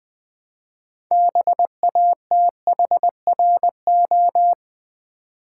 Morse code, heard as a keyed tone, about 20 wpm, 700 Hz. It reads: BATHRO
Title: Build A Word - 2 Letters At A Time - 8-Letter Words - 20wpm